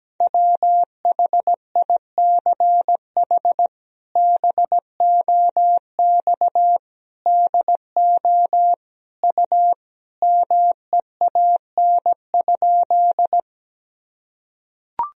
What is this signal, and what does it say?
Morse code, heard as a keyed tone, about 17 wpm, 700 Hz. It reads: WHICH BOX DO U MEAN?
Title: Sentences from Top 500 Words and Abbreviations 17wpm